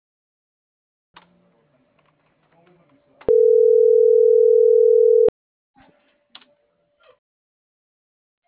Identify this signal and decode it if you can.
Ringback tone (North America), 440 + 480 Hz.
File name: CENTRAL MAYAB / out dulce.silva